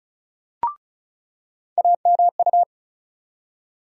Morse code, heard as a keyed tone, about 36 wpm, 700 Hz. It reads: AMU
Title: USA Call Sign Postfixes - 3 Letters 35wpm